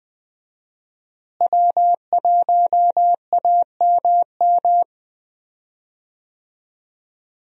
Morse code, heard as a keyed tone, about 20 wpm, 700 Hz. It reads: W1AMM
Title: USA Call Signs - 5 Characters - Repeated 3x - 20wpm